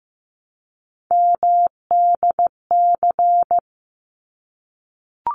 Morse code, heard as a keyed tone, about 15 wpm, 700 Hz. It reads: MDC
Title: ARRL Sections 15wpm